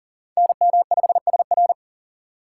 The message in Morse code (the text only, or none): NM5SR